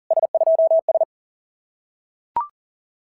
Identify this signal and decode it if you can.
Morse code, heard as a keyed tone, about 40 wpm, 680 Hz. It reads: S2S
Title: SOTA QSO Practice 40wpm